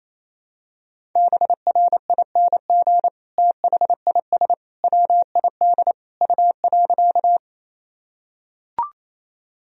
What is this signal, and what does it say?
Morse code, heard as a keyed tone, about 28 wpm, 700 Hz. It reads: BRING THIS WID U.
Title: Sentences from Top 500 Words and Abbreviations 28wpm